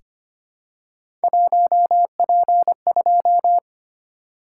Morse code, heard as a keyed tone, about 25 wpm, 700 Hz. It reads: 1P2
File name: Three Letter-Number Combinations - 25wpm